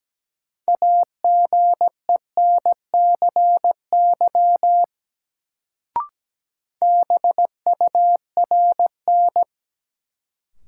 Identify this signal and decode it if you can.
Morse code, heard as a keyed tone, about 17 wpm, 700 Hz. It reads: AGENCY BURN
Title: Sentences from Top 1300 Words - New Word Drill Alt - 17wpm